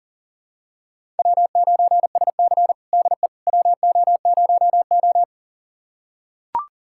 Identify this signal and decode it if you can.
Morse code, heard as a keyed tone, about 40 wpm, 700 Hz. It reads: W9SC DE WO0O
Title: USA Callsign DE Callsign - 4 Characters - 40wpm